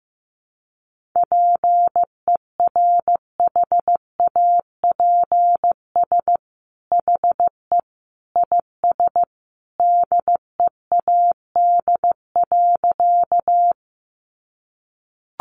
Morse code, heard as a keyed tone, about 15 wpm, 700 Hz. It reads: PERHAPS HE IS DEAD.
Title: Sentences from Top 1500 Words 15wpm